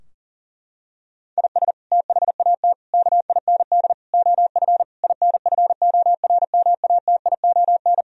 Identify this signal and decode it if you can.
Morse code, heard as a keyed tone, about 40 wpm, 700 Hz. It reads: IS THAT KIND OF INFORMATION
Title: Sentences from Top 1000 Words - Review Part 3 - 40wpm